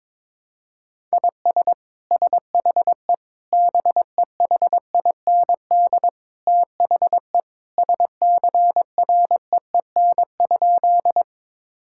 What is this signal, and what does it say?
Morse code, heard as a keyed tone, about 22 wpm, 700 Hz. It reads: IS SHE BEHIND THE SCREEN?
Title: Sentences from Top 1000 Words - Review Part 6 - 22wpm